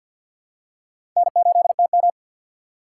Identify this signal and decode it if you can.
Morse code, heard as a keyed tone, about 52 wpm, 700 Hz. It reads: N8TM
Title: USA Call Signs - 4 Characters - 3x Character Spacing - 50wpm